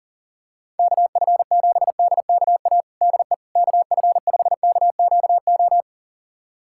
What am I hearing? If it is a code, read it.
Morse code, heard as a keyed tone, about 37 wpm, 700 Hz. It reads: KF7DKA DE KF5KQO
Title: USA Callsign DE Callsign - 6 Characters - 40wpm